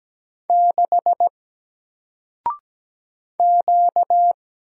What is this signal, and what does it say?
Morse code, heard as a keyed tone, about 17 wpm, 700 Hz. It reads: 6 Q
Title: Single Letter-Number 17wpm